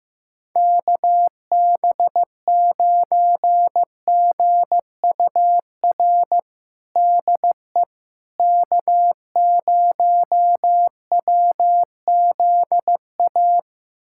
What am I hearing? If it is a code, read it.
Morse code, heard as a keyed tone, about 15 wpm, 700 Hz. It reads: KB9GUR DE K0WZA
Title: USA Callsign DE Callsign 15wpm